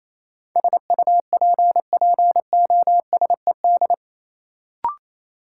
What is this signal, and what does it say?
Morse code, heard as a keyed tone, about 27 wpm, 700 Hz. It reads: SUPPOSED
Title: Sets of 3 Words - New Word Drill - 28wpm